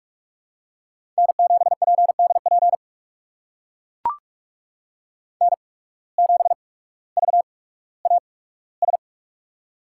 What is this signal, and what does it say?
Morse code, heard as a keyed tone, about 44 wpm, spaced as for 19 wpm, 690 Hz. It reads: N7PDP N7VAS
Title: USA Call Signs - 5 Characters - 8x Character Spacing - 45wpm